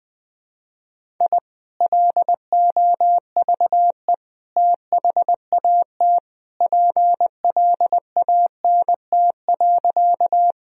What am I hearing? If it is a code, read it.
Morse code, heard as a keyed tone, about 21 wpm, 700 Hz. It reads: I LOVE THAT PLANT.
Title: Sentences from Top 1000 Words - Review Part 1 - 20wpm